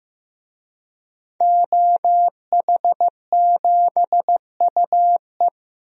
Morse code, heard as a keyed tone, about 15 wpm, 700 Hz. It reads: OH7UE
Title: DXCC-100 Call Signs - 5 Characters - Repeated 3x - 15wpm